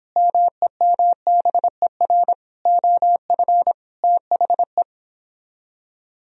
Morse code, heard as a keyed tone, about 26 wpm, 700 Hz. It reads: MEMBER OF THE
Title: Speed-Racing - Sets of 5 Words 22wpm